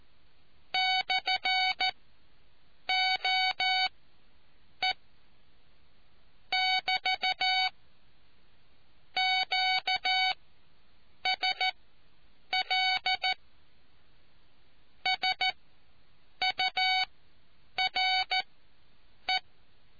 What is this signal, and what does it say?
Morse code, heard as a keyed tone, about 14 wpm, spaced as for 7 wpm, 750 Hz. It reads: /OE <BT> QSL SURE